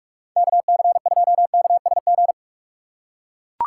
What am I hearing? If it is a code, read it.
Morse code, heard as a keyed tone, about 42 wpm, 700 Hz. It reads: KC2KSG
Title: USA Call Signs - 6 Characters - 6x Character Spacing - 45wpm